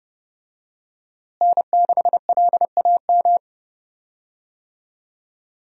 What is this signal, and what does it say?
Morse code, heard as a keyed tone, about 30 wpm, 700 Hz. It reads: N6LAM
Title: USA Call Signs - 5 Characters - Repeated 3x - 30wpm